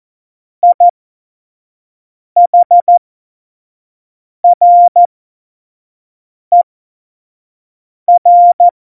Morse code, heard as a keyed tone, about 14 wpm, spaced as for 5 wpm, 700 Hz. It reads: IHRER
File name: Morsekurs / Witze / Witz 06. November 2018, 25 F70